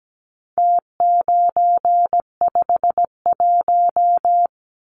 Morse code, heard as a keyed tone, about 17 wpm, 700 Hz. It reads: T951
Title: Numbers - Contest Style - 4-Digit 17wpm